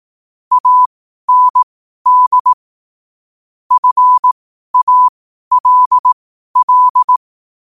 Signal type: Morse code